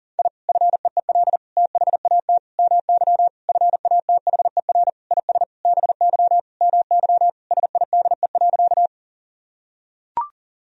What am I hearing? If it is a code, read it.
Morse code, heard as a keyed tone, about 40 wpm, 700 Hz. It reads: I FEEL THAT MY FATHER IS BY MY SIDE.
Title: Sentences from Top 1000 Words - Review Part 1 - 40wpm